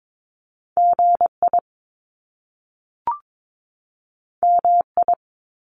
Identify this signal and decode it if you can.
Morse code, heard as a keyed tone, about 22 wpm, 700 Hz. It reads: GI MI
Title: Top Two Letter Combinations in Words 22wpm